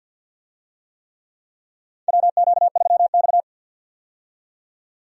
Morse code, heard as a keyed tone, about 48 wpm, 700 Hz. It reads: WQ3X
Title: USA Call Signs - 4 Characters - 50wpm